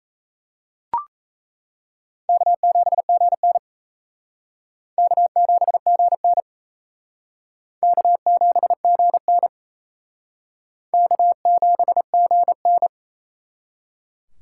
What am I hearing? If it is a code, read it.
Morse code, heard as a keyed tone, about 37 wpm, 700 Hz. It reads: K7GN K7GN K7GN K7GN